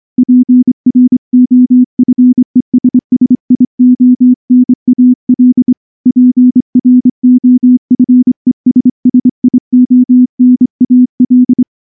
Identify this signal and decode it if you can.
Morse code, heard as a keyed tone, about 24 wpm, 250 Hz. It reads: PROFESSIONAL PROFESSIONAL
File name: morse / morse-training-25WPM